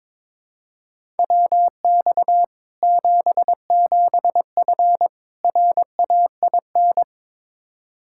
Morse code, heard as a keyed tone, about 22 wpm, 700 Hz. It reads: WX 77F RAIN
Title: Practice QSO - Exchange 2 - Info - 22wpm